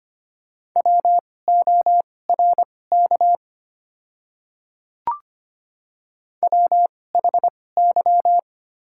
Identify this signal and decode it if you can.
Morse code, heard as a keyed tone, about 25 wpm, spaced as for 18 wpm, 700 Hz. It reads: WORK WHY
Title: 2x Character Spacing - Top 500 Words - 25wpm